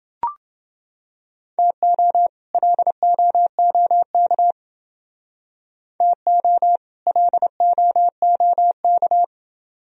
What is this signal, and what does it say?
Morse code, heard as a keyed tone, about 28 wpm, 700 Hz. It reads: TO LOOK TO LOOK